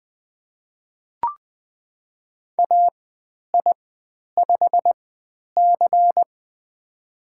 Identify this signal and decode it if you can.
Morse code, heard as a keyed tone, about 20 wpm, spaced as for 10 wpm, 700 Hz. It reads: AI5C
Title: 095 Lesson 16g - Letter c - Call Signs 20wpm